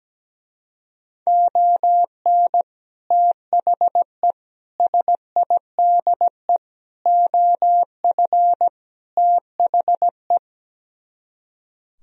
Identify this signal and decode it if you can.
Morse code, heard as a keyed tone, about 17 wpm, 700 Hz. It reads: ON THE SIDE OF THE
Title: Speed-Racing - Sets of 5 Words 17wpm